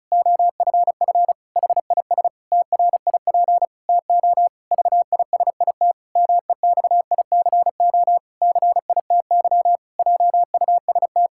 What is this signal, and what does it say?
Morse code, heard as a keyed tone, about 34 wpm, 700 Hz. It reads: OFF HIS TRIP TO VISIT MEXICO CITY JUST
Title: Pangram Challenge v2 - 35wpm